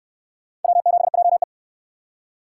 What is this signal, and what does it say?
Morse code, heard as a keyed tone, about 69 wpm, 700 Hz. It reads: W7JE